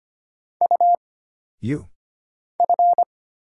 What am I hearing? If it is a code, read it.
Morse code, heard as a keyed tone, about 25 wpm, 700 Hz. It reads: U F
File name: Single Character (Letter:Number:Punctuation:Prosign) - Mind-Melt - 25wpm